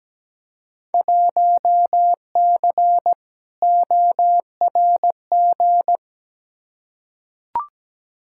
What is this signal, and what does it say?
Morse code, heard as a keyed tone, about 17 wpm, 700 Hz. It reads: 1C ORG
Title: Field Day Exchanges - 17wpm